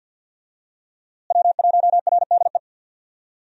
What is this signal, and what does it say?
Morse code, heard as a keyed tone, about 50 wpm, 700 Hz. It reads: W1RDE
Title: USA Call Signs - 5 Characters - Repeated 2x - 50wpm